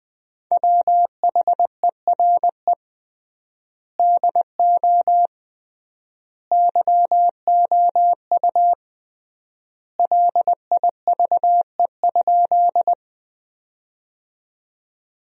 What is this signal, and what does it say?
Morse code, heard as a keyed tone, about 20 wpm, 700 Hz. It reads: WHERE DO YOU LIVE?
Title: 3x Word-Spacing - Sentences from Top 200 Words - 20wpm